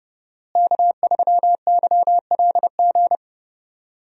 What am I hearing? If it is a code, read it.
Morse code, heard as a keyed tone, about 30 wpm, 700 Hz. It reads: K3YLG
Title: USA Call Signs - 5 Characters - Repeated 2x - 30wpm